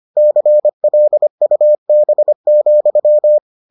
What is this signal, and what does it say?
Morse code, heard as a keyed tone, about 25 wpm, 600 Hz. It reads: CLUB,